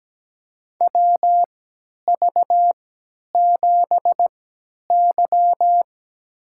Morse code, heard as a keyed tone, about 17 wpm, spaced as for 10 wpm, 700 Hz. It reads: WV7Y
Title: USA Call Signs - 3x Character Spacing - 17wpm